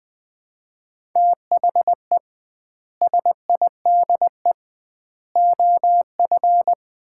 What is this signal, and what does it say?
Morse code, heard as a keyed tone, about 20 wpm, 700 Hz. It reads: THE SIDE OF
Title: 2x Word-Spacing - Sets of 4 Words - 20wpm